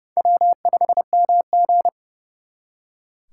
Morse code, heard as a keyed tone, about 36 wpm, 700 Hz. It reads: W5MG
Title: Speed Racing - USA 1x2 Call Signs 30wpm